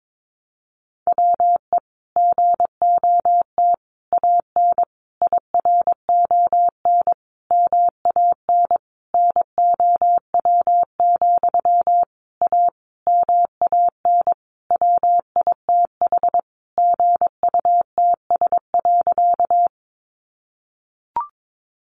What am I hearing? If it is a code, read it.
Morse code, heard as a keyed tone, about 22 wpm, 700 Hz. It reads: WE GOT AN IRON MAN NOW, A MAN WITH GUTS.
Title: A Sound Of Thunder 22wpm